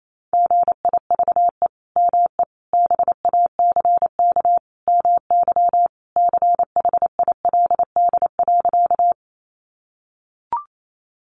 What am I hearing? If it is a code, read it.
Morse code, heard as a keyed tone, about 28 wpm, 700 Hz. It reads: GIVE ME BACK MY CHILD.
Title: Sentences from Top 800 Words 28wpm